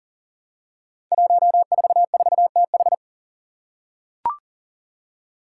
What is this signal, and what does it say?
Morse code, heard as a keyed tone, about 40 wpm, 700 Hz. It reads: 144TH
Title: Five Challenging Letter-Number Combinations 40wpm